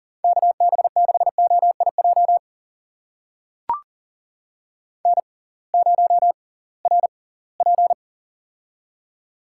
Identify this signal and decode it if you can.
Morse code, heard as a keyed tone, about 38 wpm, spaced as for 23 wpm, 700 Hz. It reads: KB6OIJ N0RP
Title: USA Call Signs - 6x Character Spacing - 40wpm